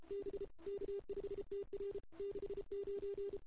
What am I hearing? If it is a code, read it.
Morse code, heard as a keyed tone, about 30 wpm, 410 Hz. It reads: B K5TR 69
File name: CW / 2008-ss-cw-k5tr-op-n5rz -02:00